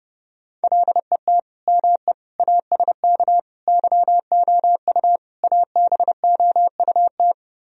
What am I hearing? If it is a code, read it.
Morse code, heard as a keyed tone, about 29 wpm, 700 Hz. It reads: LET ME ASK YOU ABOUT